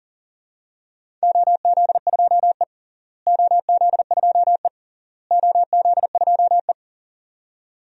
Morse code, heard as a keyed tone, about 40 wpm, 700 Hz. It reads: OZ2E OZ2E OZ2E
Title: DXCC-100 Call Signs - 4 Characters - Repeated 3x - 40wpm